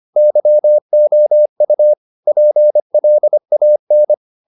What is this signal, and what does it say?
Morse code, heard as a keyed tone, about 25 wpm, 600 Hz. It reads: YOU PLAN